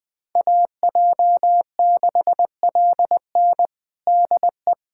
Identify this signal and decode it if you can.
Morse code, heard as a keyed tone, about 20 wpm, 700 Hz. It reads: AJ6LN DE
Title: USA Callsign DE Callsign - 5 Characters - 20wpm